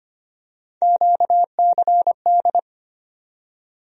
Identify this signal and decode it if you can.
Morse code, heard as a keyed tone, about 25 wpm, 700 Hz. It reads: QCD
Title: USA Call Sign Postfixes - 3 Letters 25wpm